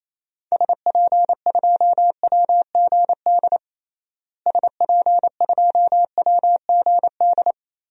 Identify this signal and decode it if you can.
Morse code, heard as a keyed tone, about 28 wpm, 700 Hz. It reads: SP2WGD SP2WGD
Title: DXCC-100 Call Signs - 6 Characters - Repeated 2x - 28wpm